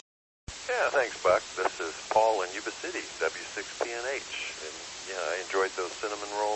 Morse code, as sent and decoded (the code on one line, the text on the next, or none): none